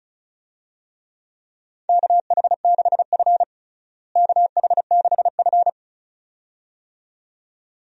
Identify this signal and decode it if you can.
Morse code, heard as a keyed tone, about 36 wpm, 700 Hz. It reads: KH6F KH6F
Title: DXCC-100 Call Signs - 4 Characters - Repeated 2x - 35wpm